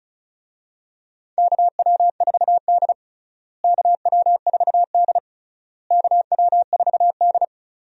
Morse code, heard as a keyed tone, about 36 wpm, 700 Hz. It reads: KW4D KW4D KW4D
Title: USA Call Signs - 4 Characters - Repeated 3x - 35wpm